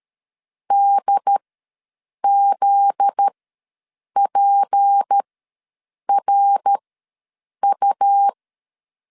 Morse code, heard as a keyed tone, about 13 wpm, spaced as for 7 wpm, 800 Hz. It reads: DZPRU
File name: LEZIONI / 9 2867 ws 12wpm 3431